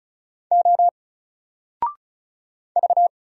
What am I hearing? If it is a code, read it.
Morse code, heard as a keyed tone, about 35 wpm, 700 Hz. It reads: O V
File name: Single-Letter-Number-35wpm